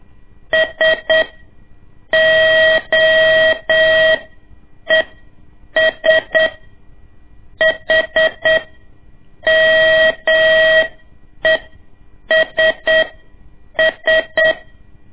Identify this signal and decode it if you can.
Morse code, heard as a keyed tone, about 8 wpm, spaced as for 6 wpm, 640 Hz. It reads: SOES HMESS
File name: ik1weg2 / CORSI CW / lez2-1